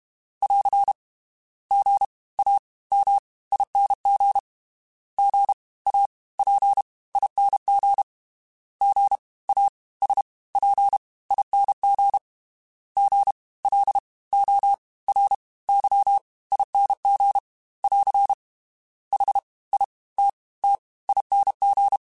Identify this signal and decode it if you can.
Morse code, heard as a keyed tone, about 32 wpm, spaced as for 21 wpm, 800 Hz. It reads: <AR> GAMING GAPING GASPING GLORYING <AR> HITTING
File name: Special INGs / Special INGs-mp3s / Fast ing @30